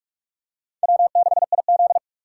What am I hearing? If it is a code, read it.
Morse code, heard as a keyed tone, about 47 wpm, 700 Hz. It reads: W6IZ
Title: USA Call Signs - 3x Character Spacing - 45wpm